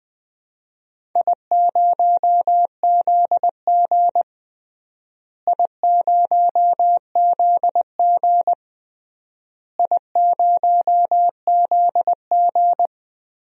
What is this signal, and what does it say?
Morse code, heard as a keyed tone, about 20 wpm, 700 Hz. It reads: I0ZG I0ZG I0ZG
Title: DXCC-100 Call Signs - 4 Characters - Repeated 3x - 20wpm